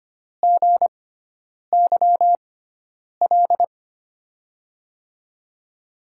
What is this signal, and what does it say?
Morse code, heard as a keyed tone, about 25 wpm, spaced as for 9 wpm, 700 Hz. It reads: GYL
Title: USA Call Signs - 6x Character Spacing - 25wpm